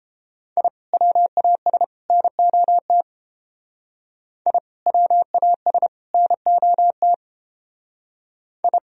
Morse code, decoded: I WAS NOT I WAS NOT I

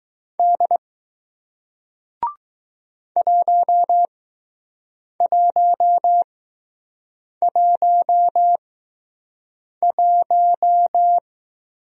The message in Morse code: D 1111

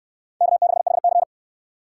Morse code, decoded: K6HC

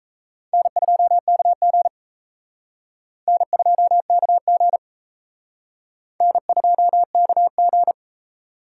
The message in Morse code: N2KG N2KG N2KG

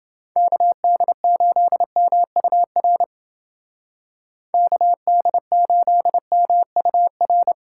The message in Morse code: KD8MUR KD8MUR

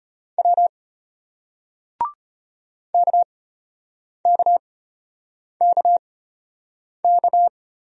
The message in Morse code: W KKKK

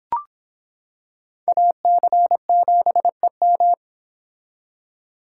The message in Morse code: AC7EM

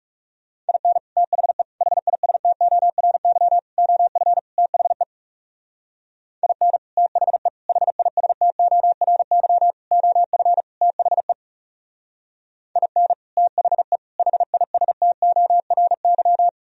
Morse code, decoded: IN THE HISTORY OF THE IN THE HISTORY OF THE IN THE HISTORY

